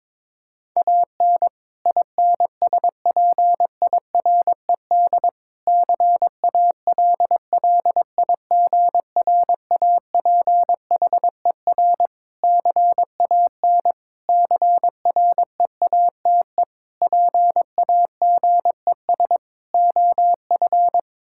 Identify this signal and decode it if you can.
Morse code, heard as a keyed tone, about 22 wpm, 700 Hz. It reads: AN INSPIRED CALLIGRAPHER CAN CREATE PAGES OF
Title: Pangram Challenge v2 - 22wpm